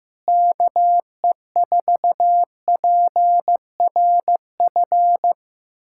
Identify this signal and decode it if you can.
Morse code, heard as a keyed tone, about 15 wpm, 700 Hz. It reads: KE4PRF
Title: USA Callsign DE Callsign - 6 Characters - 15wpm